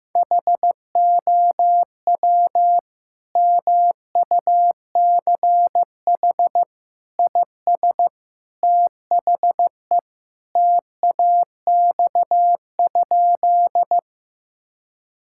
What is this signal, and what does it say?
Morse code, heard as a keyed tone, about 15 wpm, 700 Hz. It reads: HOW MUCH IS THE TAX?
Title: Sentences from Top 1000 Words - Review Part 5 - 15wpm